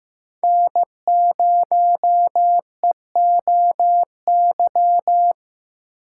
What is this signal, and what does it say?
Morse code, heard as a keyed tone, about 15 wpm, 700 Hz. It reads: N0EOY
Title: USA Call Signs - 5 Characters - 8x Character Spacing - 15wpm